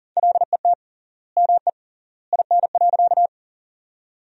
Morse code, heard as a keyed tone, about 37 wpm, 700 Hz. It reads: LET ME IN.
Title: 3x Word-Spacing - Sentences from Top 300 Words - 40wpm